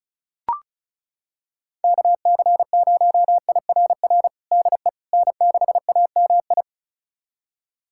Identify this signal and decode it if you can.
Morse code, heard as a keyed tone, about 36 wpm, 700 Hz. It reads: KC0IRR DE N6AMI
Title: USA Callsign DE Callsign 35wpm